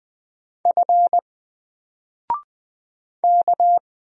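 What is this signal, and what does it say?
Morse code, heard as a keyed tone, about 20 wpm, 700 Hz. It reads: F K
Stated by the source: Single letter and number Morse drill at 20 wpm, from N4LCD.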